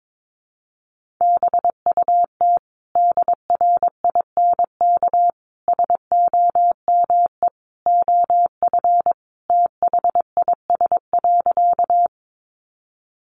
Morse code, heard as a keyed tone, about 22 wpm, 700 Hz. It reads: BUT DRINK SOME OF THIS.
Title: Sentences from Top 1500 Words 22wpm